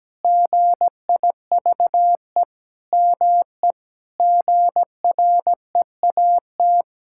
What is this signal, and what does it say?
Morse code, heard as a keyed tone, about 17 wpm, 700 Hz. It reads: GIVE ME GREAT
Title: Sentences from Top 1600 Words 17wpm